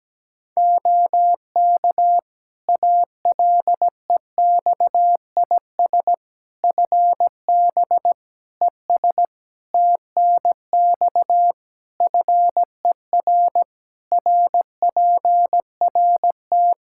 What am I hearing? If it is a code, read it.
Morse code, heard as a keyed tone, about 17 wpm, 700 Hz. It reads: OK ALEXIS FB ES TNX FER RPRT
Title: Practice QSO - Exchange 2 - Info - 17wpm